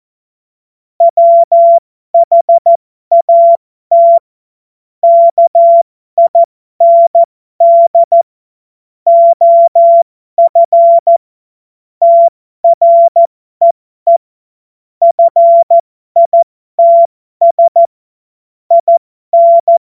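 Morse code, heard as a keyed tone, about 14 wpm, spaced as for 12 wpm, 680 Hz. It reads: WHAT KIND OF TREE FITS IN
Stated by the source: Amateur radio CW practice text made of jokes.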